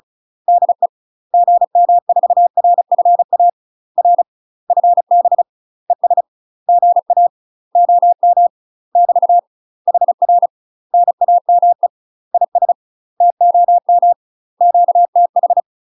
Morse code, heard as a keyed tone, about 36 wpm, 700 Hz. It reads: DE GM4RFA R FB ES GA OM <BT> HR NAME IS TOM QTH
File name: UK MASTER QSOs TRAINING / 35WPM / UK MASTER QSOs 0035